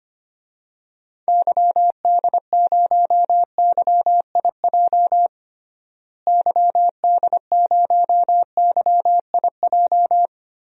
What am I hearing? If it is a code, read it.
Morse code, heard as a keyed tone, about 25 wpm, 700 Hz. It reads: YD0YIJ YD0YIJ
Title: DXCC-100 Call Signs - 6 Characters - Repeated 3x - 25wpm